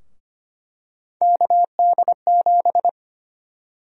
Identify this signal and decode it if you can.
Morse code, heard as a keyed tone, about 25 wpm, 700 Hz. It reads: KD7
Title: USA Call Sign Prefixes - 25wpm